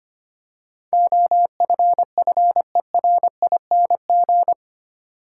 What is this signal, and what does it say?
Morse code, heard as a keyed tone, about 25 wpm, 700 Hz. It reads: OFFERING
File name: Eight Letter Words - 25wpm